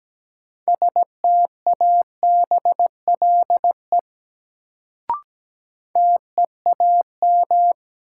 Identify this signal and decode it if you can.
Morse code, heard as a keyed tone, about 17 wpm, 700 Hz. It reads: STABLE TEAM
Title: Sentences from Top 1500 Words - New Word Drill - 17wpm